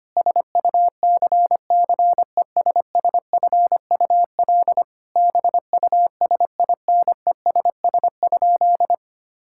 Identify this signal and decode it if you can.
Morse code, heard as a keyed tone, about 25 wpm, 700 Hz. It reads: SUCCESSFUL BUSINESS?
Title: Sentences from Top 1000 Words - Review Part 6 - 25wpm